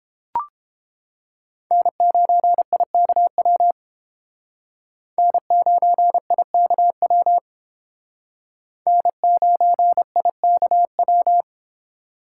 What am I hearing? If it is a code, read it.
Morse code, heard as a keyed tone, about 32 wpm, 700 Hz. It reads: N9IKW N9IKW N9IKW